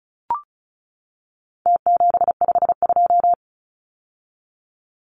Morse code, heard as a keyed tone, about 34 wpm, 700 Hz. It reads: T752
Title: Numbers - Contest Style - 4-Digit 35wpm